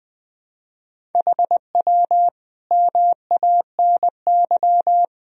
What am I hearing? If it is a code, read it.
Morse code, heard as a keyed tone, about 20 wpm, 700 Hz. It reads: HW MANY
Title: Sentences from Top 400 Words and Abbreviations 20wpm